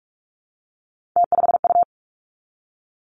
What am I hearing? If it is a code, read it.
Morse code, heard as a keyed tone, about 46 wpm, 700 Hz. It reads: T5U
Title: Three Challenging Letter-Number Combinations 45wpm